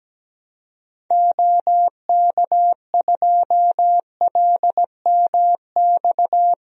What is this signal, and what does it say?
Morse code, heard as a keyed tone, about 17 wpm, 700 Hz. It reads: OK2LMX
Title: DXCC-100 Call Signs - 6 Characters - Repeated 3x - 17wpm